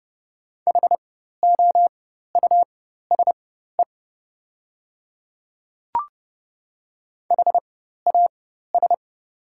Morse code, heard as a keyed tone, about 30 wpm, spaced as for 14 wpm, 700 Hz. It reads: HOUSE HAS
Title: 4x Character Spacing - Top 200 Words - 30wpm